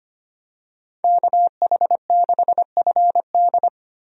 Morse code, heard as a keyed tone, about 25 wpm, 690 Hz. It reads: KH6FD
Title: USA Call Signs - 5 Characters - 25wpm